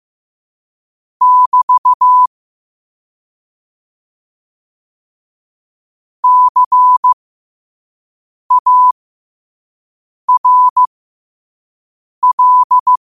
Morse code, heard as a keyed tone, about 15 wpm, spaced as for 5 wpm, 1000 Hz. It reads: <BT> CARL
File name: qotdfiles / 5wpm